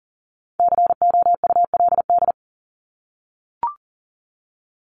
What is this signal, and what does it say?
Morse code, heard as a keyed tone, about 40 wpm, 700 Hz. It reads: COULD
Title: Top 200 Words Repeated 40wpm